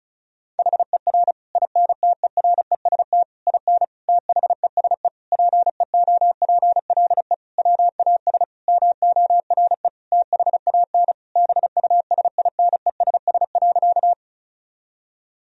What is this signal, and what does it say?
Morse code, heard as a keyed tone, about 36 wpm, 700 Hz. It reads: HER INTEREST IN THESE PEOPLE WAS MORE THAN BUSINESS.
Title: Sentences from Top 1000 Words - Review Part 2 - 35wpm